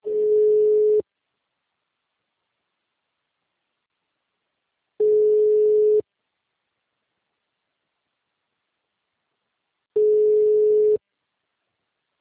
Ringback tone (Europe, 425 Hz).